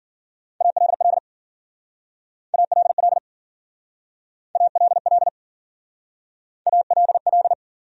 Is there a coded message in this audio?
Yes: Morse code from 0.6 to 7.5 s, heard as a keyed tone, about 50 wpm, 700 Hz. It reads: ALL ALL ALL ALL